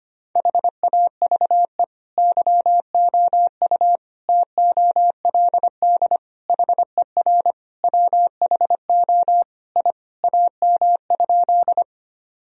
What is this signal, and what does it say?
Morse code, heard as a keyed tone, about 25 wpm, 700 Hz. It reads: HAVE YOU TOLD HER WHO I AM?
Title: Sentences from Top 1000 Words - Review Part 2 - 25wpm